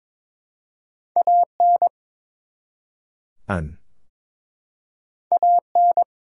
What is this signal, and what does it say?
Morse code, heard as a keyed tone, about 22 wpm, 700 Hz. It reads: AN AN
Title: Two Letter Words - 22wpm